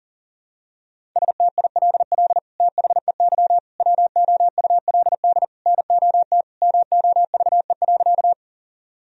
Morse code, heard as a keyed tone, about 40 wpm, 700 Hz. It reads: STILL THEY WOULD NOT MOVE.
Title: Sentences from Top 1000 Words - Review Part 1 - 40wpm